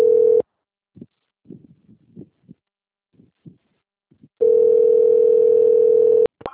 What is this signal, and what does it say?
Ringback tone (North America), 440 + 480 Hz.
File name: out- - - - - .5769